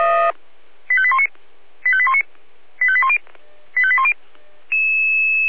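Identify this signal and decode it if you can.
Five-tone selective call (ZVEI), heard as quick runs of five tones, tones of about 70 ms. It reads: 85819 85819 85810 85810
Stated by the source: ZVEI five-tone selective call 85840 received via SDR.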